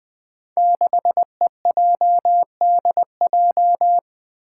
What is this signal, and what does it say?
Morse code, heard as a keyed tone, about 20 wpm, 700 Hz. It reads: 6EJDJ
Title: Five Challenging Letter-Number Combinations 20wpm